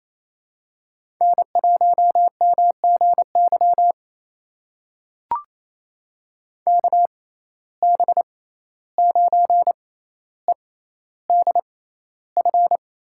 Morse code, heard as a keyed tone, about 28 wpm, spaced as for 14 wpm, 700 Hz. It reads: N1MGY KB9EDF
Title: USA Call Signs - 6x Character Spacing - 28wpm